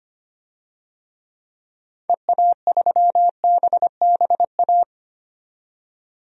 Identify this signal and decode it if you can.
Morse code, heard as a keyed tone, about 25 wpm, 700 Hz. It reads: EA3BBA
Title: DXCC-100 Call Signs - 6 Characters - 25wpm